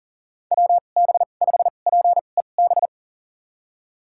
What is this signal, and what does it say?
Morse code, heard as a keyed tone, about 40 wpm, spaced as for 29 wpm, 700 Hz. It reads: WB5PEB
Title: USA Call Signs - 6 Characters - 2x Character Spacing - 40wpm